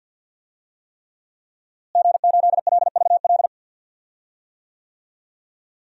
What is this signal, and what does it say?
Morse code, heard as a keyed tone, about 53 wpm, 700 Hz. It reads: G8LVL
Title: DXCC-100 Call Signs - 5 Characters - Repeated 3x - 50wpm